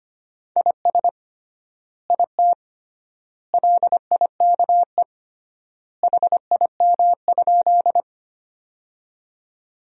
Morse code, heard as a keyed tone, about 25 wpm, 700 Hz. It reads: IS IT LIKE HIM?
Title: 3x Word-Spacing - Sentences from Top 100 Words - 25wpm